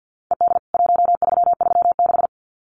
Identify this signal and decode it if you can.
Morse code, heard as a keyed tone, about 49 wpm, 700 Hz. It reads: ED 1336
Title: CWT Exchange 50wpm